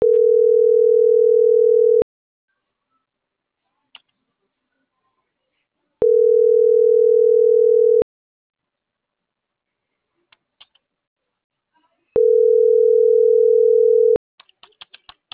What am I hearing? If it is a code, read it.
Ringback tone (North America), 440 + 480 Hz.